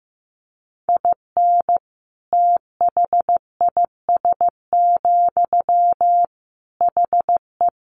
Morse code, heard as a keyed tone, about 15 wpm, 700 Hz. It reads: IN THIS, HE
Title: Sentences from Top 1500 Words 15wpm